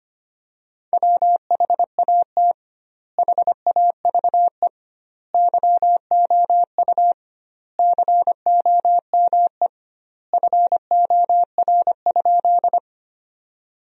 Morse code, heard as a keyed tone, about 25 wpm, 700 Hz. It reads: WHAT HAVE YOU COME FOR?
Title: 2x Word-Spacing - Sentences from Top 100 Words - 25wpm